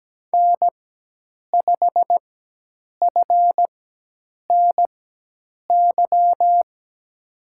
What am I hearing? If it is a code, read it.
Morse code, heard as a keyed tone, about 17 wpm, spaced as for 8 wpm, 700 Hz. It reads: N5FNY
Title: USA Call Signs - 5 Characters - 4x Character Spacing - 17wpm